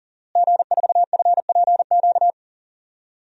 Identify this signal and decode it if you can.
Morse code, heard as a keyed tone, about 40 wpm, 700 Hz. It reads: G4FPQ
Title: DXCC-100 Call Signs - 5 Characters - Repeated 3x - 40wpm